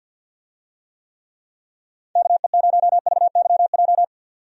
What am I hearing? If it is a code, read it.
Morse code, heard as a keyed tone, about 50 wpm, 700 Hz. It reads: KE0VYJ
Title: USA Call Signs - 6 Characters - 50wpm